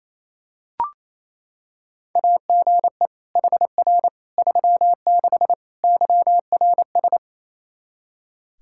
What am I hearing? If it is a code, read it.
Morse code, heard as a keyed tone, about 28 wpm, 700 Hz. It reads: AGE HR 36 YRS
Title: Practice QSO - Exchange 3 - More Info - 28wpm